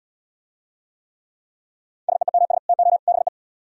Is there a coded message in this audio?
Yes: Morse code from 2.1 to 3.3 s, heard as a keyed tone, about 72 wpm, 700 Hz. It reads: SEEMS TO BE